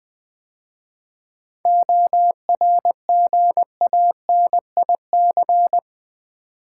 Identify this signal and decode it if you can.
Morse code, heard as a keyed tone, about 20 wpm, 700 Hz. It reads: ORGANIC
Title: 7 Letter Words - First 3 Letters Given - 20wpm